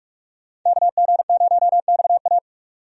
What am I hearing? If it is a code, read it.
Morse code, heard as a keyed tone, about 44 wpm, 700 Hz. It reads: KG0XA
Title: USA Call Signs - 5 Characters - 45wpm